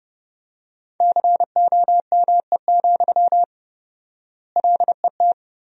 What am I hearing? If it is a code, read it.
Morse code, heard as a keyed tone, about 30 wpm, 700 Hz. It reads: COME, LET
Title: 4x Word-Spacing - Sentences from Top 300 Words - 30wpm